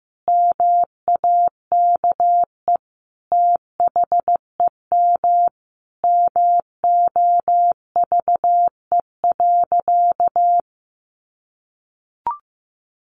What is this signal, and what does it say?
Morse code, heard as a keyed tone, about 15 wpm, 700 Hz. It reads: MAKE THEM MOVE.
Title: A Sound Of Thunder 15wpm